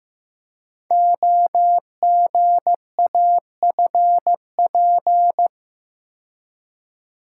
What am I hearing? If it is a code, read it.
Morse code, heard as a keyed tone, about 15 wpm, 700 Hz. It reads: OGAFP
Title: Code Groups - 5 Letters - 15wpm